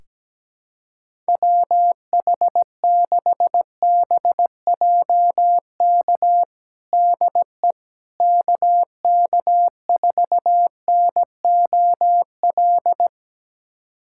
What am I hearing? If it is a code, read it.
Morse code, heard as a keyed tone, about 17 wpm, 700 Hz. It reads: WH6BJK DE KK4NOL
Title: USA Callsign DE Callsign 17wpm